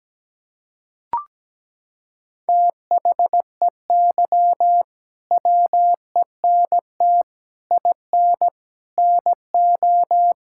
Morse code, heard as a keyed tone, about 17 wpm, 700 Hz. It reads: THEY WENT IN NO
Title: Sentences from Top 1000 Words - Review Part 5 - 17wpm